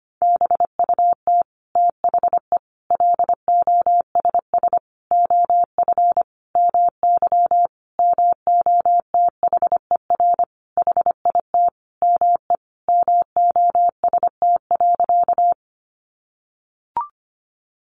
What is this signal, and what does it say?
Morse code, heard as a keyed tone, about 25 wpm, 700 Hz. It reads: BUT THE LOSS OF MY MOTHER HIT ME MOST.
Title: Sentences from Top 1400 Words 25wpm